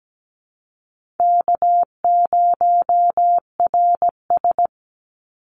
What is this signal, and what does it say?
Morse code, heard as a keyed tone, about 17 wpm, 700 Hz. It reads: K0RS
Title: NA SOTA Chaser Callsigns 17wpm